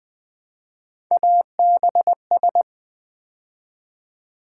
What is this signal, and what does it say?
Morse code, heard as a keyed tone, about 20 wpm, 700 Hz. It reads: ABS
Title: Build A Word - 3 Letters At A Time - 8-Letter Words - 20wpm